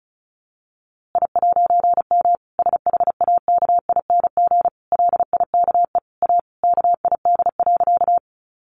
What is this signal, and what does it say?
Morse code, heard as a keyed tone, about 34 wpm, 700 Hz. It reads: I'M SHAKING LIKE A KID.
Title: A Sound Of Thunder 35wpm